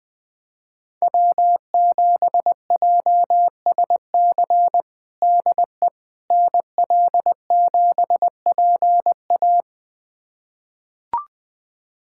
Morse code, heard as a keyed tone, about 20 wpm, 700 Hz. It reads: W7JSC DE NL7PA
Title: USA Callsign DE Callsign - 5 Characters - 20wpm